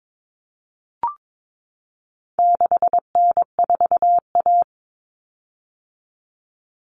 Morse code, heard as a keyed tone, about 22 wpm, 700 Hz. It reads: 6N4A
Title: Common Cut Numbers - 4-Digit 22wpm